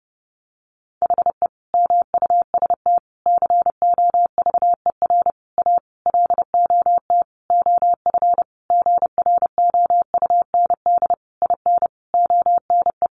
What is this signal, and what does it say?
Morse code, heard as a keyed tone, about 30 wpm, 700 Hz. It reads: HE MUST COVER A LOT OF GROUND IN ONE
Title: Sentences from Top 500 Words 30wpm